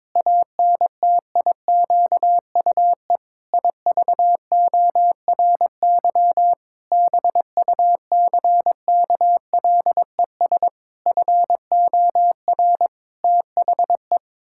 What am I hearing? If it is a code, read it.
Morse code, heard as a keyed tone, about 22 wpm, 700 Hz. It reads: ANTIQUE IVORY BUCKLES FOR THE
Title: Pangram Challenge v2 - 22wpm